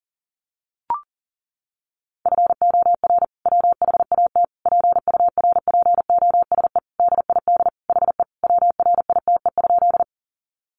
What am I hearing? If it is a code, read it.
Morse code, heard as a keyed tone, about 37 wpm, 700 Hz. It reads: FOR WHAT PURPOSE DID HE WRITE?
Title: Sentences from Top 800 Words 40wpm